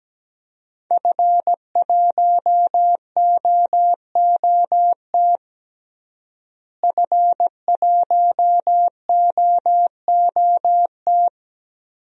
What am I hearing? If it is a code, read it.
Morse code, heard as a keyed tone, about 17 wpm, 700 Hz. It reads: F1OOT F1OOT
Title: DXCC-100 Call Signs - 5 Characters - Repeated 2x - 17wpm